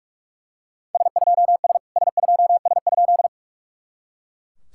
Morse code, heard as a keyed tone, about 39 wpm, 690 Hz. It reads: S2S S2S?
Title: SOTA QSO Practice 45wpm